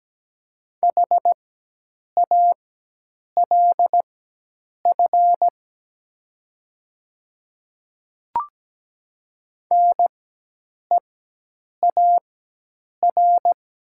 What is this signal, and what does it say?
Morse code, heard as a keyed tone, about 17 wpm, spaced as for 8 wpm, 700 Hz. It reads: HALF NEAR
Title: 4x Character Spacing - Top 500 Words - 17wpm